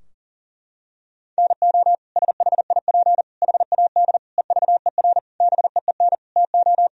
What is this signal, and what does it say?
Morse code, heard as a keyed tone, about 40 wpm, 700 Hz. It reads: NO SHIP HAD EVER BEEN TO
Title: Sentences from Top 1000 Words - Review Part 4 - 40wpm